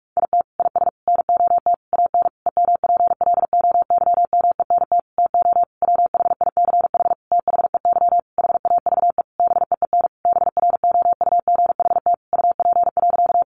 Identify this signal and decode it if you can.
Morse code, heard as a keyed tone, about 44 wpm, 700 Hz. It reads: IT IS NOT AN EMPLOYMENT TO WHICH THEY HAVE BEEN BROUGHT UP.
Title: Sentences from Top 1500 Words 45wpm